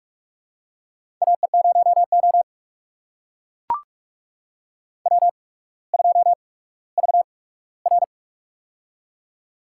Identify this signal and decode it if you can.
Morse code, heard as a keyed tone, about 44 wpm, spaced as for 19 wpm, 700 Hz. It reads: AE0O W2VR
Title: USA Call Signs - 4 Characters - 8x Character Spacing - 45wpm